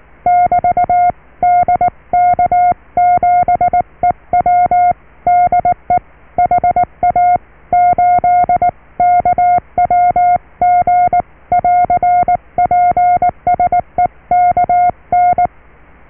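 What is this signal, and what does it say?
Morse code, heard as a keyed tone, about 19 wpm, 710 Hz. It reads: <BT> DK7EW DE HA8KWG <AR> PSE KN
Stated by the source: Morse QSO on 14 MHz with a German station, received at RST 579.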